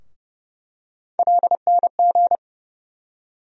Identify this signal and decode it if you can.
Morse code, heard as a keyed tone, about 30 wpm, 700 Hz. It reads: LNG
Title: USA Call Sign Postfixes - 3 Letters 30wpm